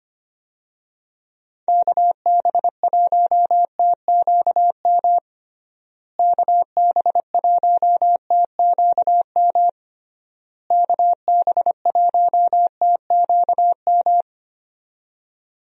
Morse code, heard as a keyed tone, about 25 wpm, 700 Hz. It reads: KB1TQM KB1TQM KB1TQM
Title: USA Call Signs - 6 Characters - Repeated 3x - 25wpm